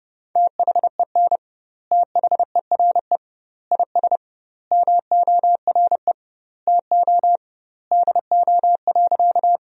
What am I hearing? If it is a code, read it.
Morse code, heard as a keyed tone, about 30 wpm, 700 Hz. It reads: THEN THERE IS MORE TO DO.
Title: 2x Word-Spacing - Sentences from Top 100 Words - 30wpm